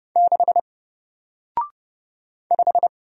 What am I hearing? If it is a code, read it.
Morse code, heard as a keyed tone, about 30 wpm, 700 Hz. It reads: 6 5
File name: Single-Letter-Number-30wpm